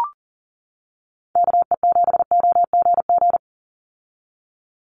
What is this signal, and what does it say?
Morse code, heard as a keyed tone, about 40 wpm, 700 Hz. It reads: KE7OGG
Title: USA Call Signs 40wpm